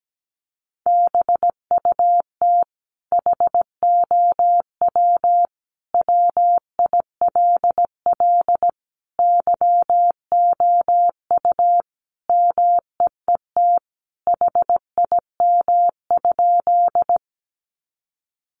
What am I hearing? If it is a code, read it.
Morse code, heard as a keyed tone, about 17 wpm, 700 Hz. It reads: BUT HOW WILL YOU MEET HIM?
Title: Sentences from Top 1100 Words 17wpm